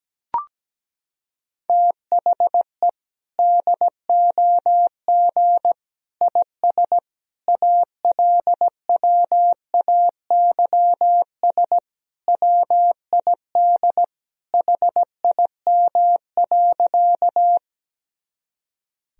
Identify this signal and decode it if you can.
Morse code, heard as a keyed tone, about 17 wpm, 700 Hz. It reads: THE DOG IS ALWAYS WID HIM.
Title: Sentences from Top 400 Words and Abbreviations 17wpm